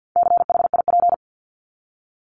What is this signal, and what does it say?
Morse code, heard as a keyed tone, about 50 wpm, 700 Hz. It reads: CHIP
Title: Top 1751-2000 Words 50wpm